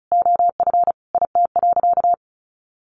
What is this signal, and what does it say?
Morse code, heard as a keyed tone, about 34 wpm, 700 Hz. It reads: OF IT.
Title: Sentences from Top 1200 Words 35wpm